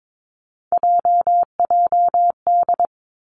Morse code, heard as a keyed tone, about 22 wpm, 700 Hz. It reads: JJD
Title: Three Challenging Letter-Number Combinations 22wpm